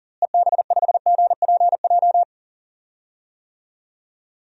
Morse code, heard as a keyed tone, about 39 wpm, 700 Hz. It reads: EB5GPJ
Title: DXCC-100 Call Signs - 6 Characters - Repeated 2x - 40wpm